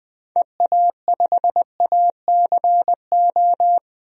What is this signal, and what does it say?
Morse code, heard as a keyed tone, about 20 wpm, 700 Hz. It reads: EA5ACO
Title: DXCC-100 Call Signs - 6 Characters - 20wpm